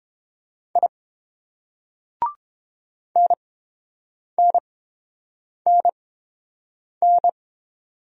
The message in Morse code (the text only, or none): I NNNN